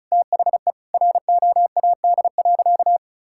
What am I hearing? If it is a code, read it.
Morse code, heard as a keyed tone, about 34 wpm, 700 Hz. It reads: THE ROAD.
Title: Sentences from Top 1000 Words - Review Part 5 - 35wpm